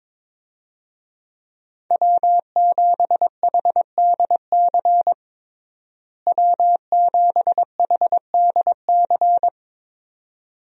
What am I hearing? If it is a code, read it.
Morse code, heard as a keyed tone, about 21 wpm, 700 Hz. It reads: W7HDC W7HDC
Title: USA Call Signs - 5 Characters - Repeated 2x - 22wpm